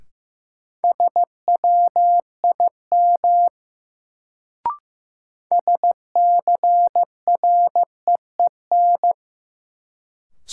Morse code, heard as a keyed tone, about 15 wpm, 700 Hz. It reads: SWIM SCREEN
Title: Sentences from Top 1000 Words - Review Part 6 - New Word Drill - 15wpm